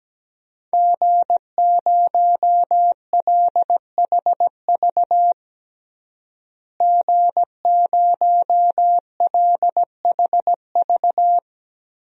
Morse code, heard as a keyed tone, about 17 wpm, 700 Hz. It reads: G0LHV G0LHV
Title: DXCC-100 Call Signs - 5 Characters - Repeated 3x - 17wpm